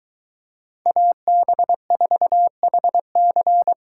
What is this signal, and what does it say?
Morse code, heard as a keyed tone, about 23 wpm, 700 Hz. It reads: AB4HC